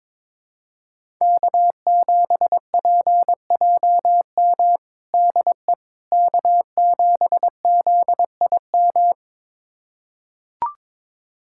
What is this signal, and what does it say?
Morse code, heard as a keyed tone, about 21 wpm, 700 Hz. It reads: K7PJM DE K7ZIM
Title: USA Callsign DE Callsign - 5 Characters - 22wpm